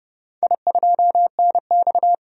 Morse code, heard as a keyed tone, about 31 wpm, 700 Hz. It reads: I2NX
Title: DXCC-100 Call Signs - 4 Characters - 30wpm